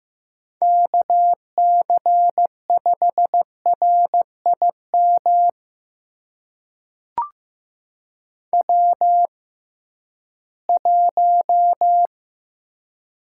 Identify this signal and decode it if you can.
Morse code, heard as a keyed tone, about 15 wpm, spaced as for 11 wpm, 700 Hz. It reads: KC5RIM W1
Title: USA Call Signs - 6x Character Spacing - 15wpm